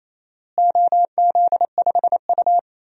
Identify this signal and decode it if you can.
Morse code, heard as a keyed tone, about 28 wpm, 700 Hz. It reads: OZ5U
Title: DXCC-100 Call Signs - 4 Characters - Repeated 3x - 28wpm